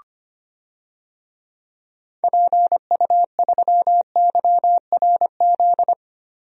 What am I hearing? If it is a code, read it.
Morse code, heard as a keyed tone, about 25 wpm, 700 Hz. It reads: PU3YRZ
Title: DXCC-100 Call Signs - 6 Characters - Repeated 3x - 25wpm